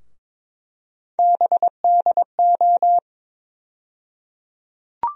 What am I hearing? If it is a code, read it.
Morse code, heard as a keyed tone, about 22 wpm, 700 Hz. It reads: BDO
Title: USA Call Sign Postfixes - 3 Letters 22wpm